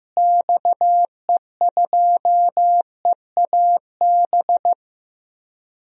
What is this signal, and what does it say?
Morse code, heard as a keyed tone, about 16 wpm, 700 Hz. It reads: XE2EAB
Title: DXCC-100 Call Signs - 6 Characters - Repeated 3x - 15wpm